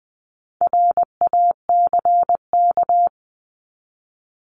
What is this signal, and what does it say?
Morse code, heard as a keyed tone, about 20 wpm, 700 Hz. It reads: RACK
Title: 260 Lesson 42e - Words 20-20wpm